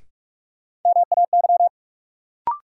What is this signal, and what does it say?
Morse code, heard as a keyed tone, about 45 wpm, 700 Hz. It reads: MAY
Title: Sentences from Top 100 Words - New Word Drill - 45wpm